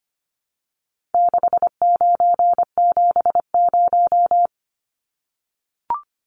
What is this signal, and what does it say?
Morse code, heard as a keyed tone, about 25 wpm, 700 Hz. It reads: 6970
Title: Numbers - 4-Digit 25wpm